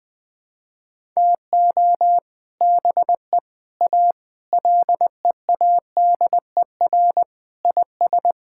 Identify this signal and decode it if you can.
Morse code, heard as a keyed tone, about 20 wpm, 700 Hz. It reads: TO BE A LEADER IS
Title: Sentences with 3-Word Phrases - Spoken Beginning - 20wpm